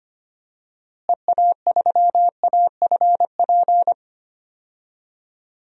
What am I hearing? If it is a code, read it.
Morse code, heard as a keyed tone, about 25 wpm, 700 Hz. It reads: EA3AFP
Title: DXCC-100 Call Signs - 6 Characters - Repeated 2x - 25wpm